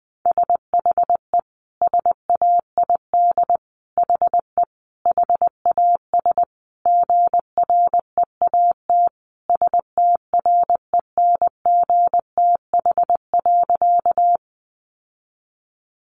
Morse code, heard as a keyed tone, about 19 wpm, 700 Hz. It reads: SHE SAID HE HAS GREAT STRENGTH.
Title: Sentences from Top 1300 Words 20wpm